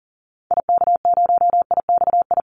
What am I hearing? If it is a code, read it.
Morse code, heard as a keyed tone, about 40 wpm, 700 Hz. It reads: IK0IXI
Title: Call Signs Repeated 3x - 40wpm